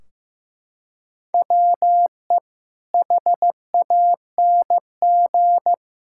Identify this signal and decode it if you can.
Morse code, heard as a keyed tone, about 15 wpm, 700 Hz. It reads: WE HANG
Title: Sentences from Top 1600 Words 15wpm